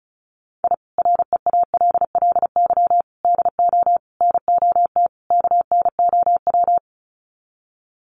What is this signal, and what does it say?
Morse code, heard as a keyed tone, about 34 wpm, 700 Hz. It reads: I REALLY DO NOT KNOW
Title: Sets of 5 Words Encore 35wpm